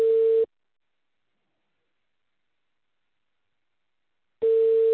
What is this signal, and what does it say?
Ringback tone (China, 450 Hz).